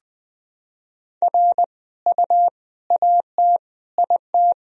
Morse code, heard as a keyed tone, about 20 wpm, 700 Hz. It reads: R U AT IT
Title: Sentences from Top 200 Words and Abbreviations 20wpm